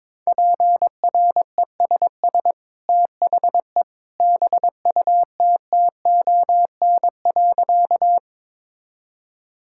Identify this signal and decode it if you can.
Morse code, heard as a keyed tone, about 22 wpm, 700 Hz. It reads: PRESS THE BUTTON.
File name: practice / 22wpm / Easy Sentences with Hints - 22wpm